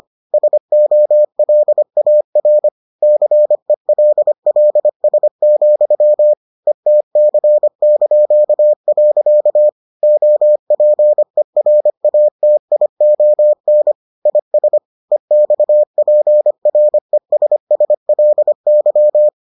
Morse code, heard as a keyed tone, about 25 wpm, 600 Hz. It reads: SOLAR CELLS, ETC). OPERATION IS EXPRESSLY